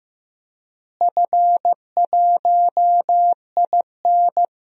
Morse code, heard as a keyed tone, about 15 wpm, 700 Hz. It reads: F1IN
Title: DXCC-100 Call Signs - 4 Characters - Repeated 3x - 15wpm